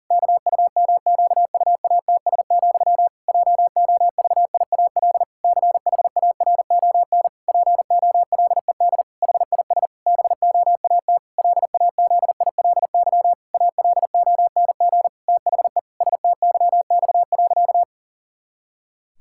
Morse code, heard as a keyed tone, about 40 wpm, 700 Hz. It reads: KUMQUATS, JOVIAL CHARON POLED HIS BOAT LAZILY ALONG THE STYX.
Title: Pangram Challenge v2 - 40wpm